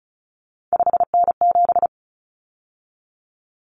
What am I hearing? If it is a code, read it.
Morse code, heard as a keyed tone, about 34 wpm, 700 Hz. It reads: 5N7
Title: Common Cut Numbers - 3-Digit 35wpm